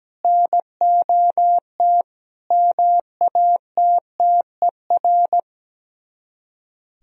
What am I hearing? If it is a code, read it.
Morse code, heard as a keyed tone, about 17 wpm, 700 Hz. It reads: NOT MATTER
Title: Speed-Racing - Sets of 4 Words 17wpm